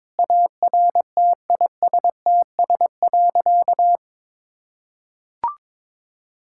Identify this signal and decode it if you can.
Morse code, heard as a keyed tone, about 22 wpm, 700 Hz. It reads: ARTISTS.
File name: practice / 22wpm / Sentences with 8-Word Phrases - 22wpm